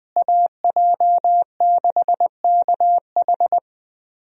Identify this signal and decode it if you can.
Morse code, heard as a keyed tone, about 21 wpm, 700 Hz. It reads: AJ6KH
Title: USA Call Signs - 5 Characters - Repeated 3x - 20wpm